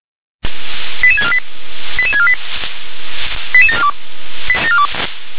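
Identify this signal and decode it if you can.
Five-tone selective call (ZVEI), heard as quick runs of five tones, tones of about 70 ms. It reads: 88648 88648 88742
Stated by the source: ZVEI five-tone selective call received on an SDR.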